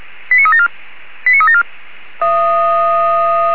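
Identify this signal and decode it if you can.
Five-tone selective call (ZVEI), heard as quick runs of five tones, tones of about 70 ms. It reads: 89374 89374
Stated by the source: ZVEI five-tone selective call received by SDR.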